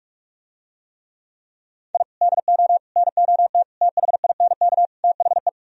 Morse code, heard as a keyed tone, about 46 wpm, 700 Hz. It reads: I DO NOT THINK THE